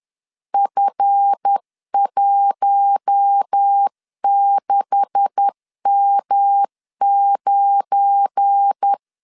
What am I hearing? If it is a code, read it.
Morse code, heard as a keyed tone, about 10 wpm, 800 Hz. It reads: F16M9
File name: LEZIONI / 9 286 10wpm 2735